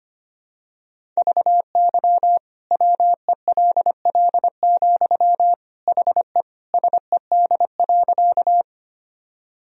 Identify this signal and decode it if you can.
Morse code, heard as a keyed tone, about 25 wpm, 700 Hz. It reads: VY WELL, HE SED.
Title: Sentences from Top 200 Words and Abbreviations 25wpm